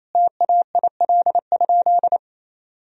Morse code, heard as a keyed tone, about 27 wpm, 700 Hz. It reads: TAIL?
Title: Sentences from Top 1000 Words - Review Part 3 - 28wpm